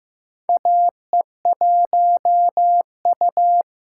Morse code, heard as a keyed tone, about 15 wpm, 700 Hz. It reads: AE1U
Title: USA Call Signs - 3x Character Spacing - 15wpm